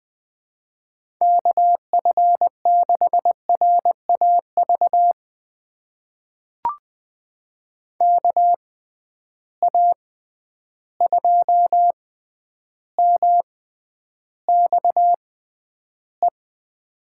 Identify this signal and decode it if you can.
Morse code, heard as a keyed tone, about 20 wpm, spaced as for 10 wpm, 700 Hz. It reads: KF6RAV KA2MXE
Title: USA Call Signs - 6x Character Spacing - 20wpm